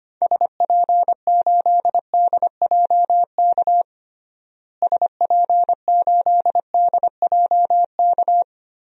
Morse code, heard as a keyed tone, about 25 wpm, 700 Hz. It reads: SP8DJK SP8DJK
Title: DXCC-100 Call Signs - 6 Characters - Repeated 3x - 25wpm